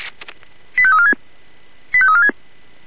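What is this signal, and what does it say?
Five-tone selective call (ZVEI), heard as quick runs of five tones, tones of about 70 ms. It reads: 85346 85346
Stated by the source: ZVEI five-tone call received on SDR.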